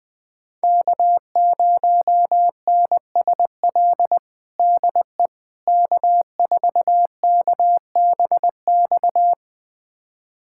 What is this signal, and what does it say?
Morse code, heard as a keyed tone, about 20 wpm, 700 Hz. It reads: K0NSL DE K4KBX
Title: USA Callsign DE Callsign - 5 Characters - 20wpm